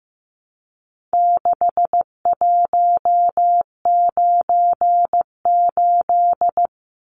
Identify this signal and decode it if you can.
Morse code, heard as a keyed tone, about 15 wpm, 700 Hz. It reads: 6198
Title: Numbers - 4-Digit 15wpm